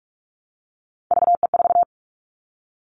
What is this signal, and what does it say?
Morse code, heard as a keyed tone, about 46 wpm, 700 Hz. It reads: VE4
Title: Callsign Prefixes for Top 25 DXCC - 45wpm